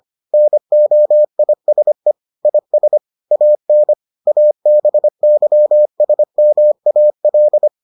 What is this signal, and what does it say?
Morse code, heard as a keyed tone, about 25 wpm, 600 Hz. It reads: NOISE IS AN ABYSMAL